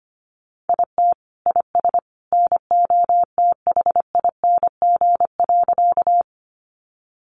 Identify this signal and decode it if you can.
Morse code, heard as a keyed tone, about 25 wpm, 700 Hz. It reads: IT IS NOTHING.
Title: Sentences from Top 500 Words 25wpm